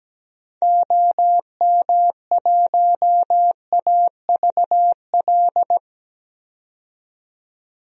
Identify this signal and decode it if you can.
Morse code, heard as a keyed tone, about 17 wpm, 700 Hz. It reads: OM1AVL
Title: DXCC-100 Call Signs - 6 Characters - 17wpm